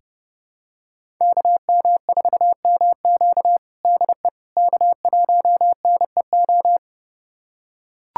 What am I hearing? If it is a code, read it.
Morse code, heard as a keyed tone, about 32 wpm, 700 Hz. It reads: KM4MQ DE K1NEO
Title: USA Callsign DE Callsign - 5 Characters - 30wpm